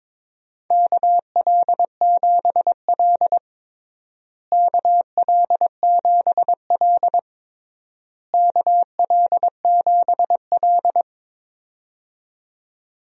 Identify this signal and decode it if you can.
Morse code, heard as a keyed tone, about 22 wpm, 700 Hz. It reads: KL7L KL7L KL7L
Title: DXCC-100 Call Signs - 4 Characters - Repeated 3x - 22wpm